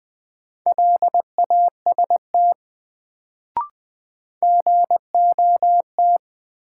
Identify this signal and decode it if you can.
Morse code, heard as a keyed tone, about 20 wpm, 700 Hz. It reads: LAST GOT
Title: Sentences from Top 300 Words - New Word Drill - 20wpm